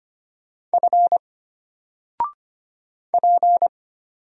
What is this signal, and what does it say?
Morse code, heard as a keyed tone, about 25 wpm, 700 Hz. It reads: F P
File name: Single-Letter-Number-25wpm